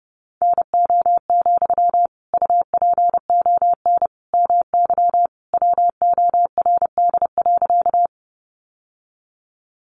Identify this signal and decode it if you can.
Morse code, heard as a keyed tone, about 30 wpm, 700 Hz. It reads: NO, UPON MY WORD.
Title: Sentences from Top 1100 Words 30wpm